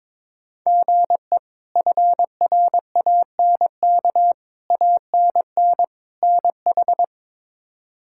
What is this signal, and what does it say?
Morse code, heard as a keyed tone, about 21 wpm, 700 Hz. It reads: GE FRANK ANN NH
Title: K1USN SST QSO Practice - 22wpm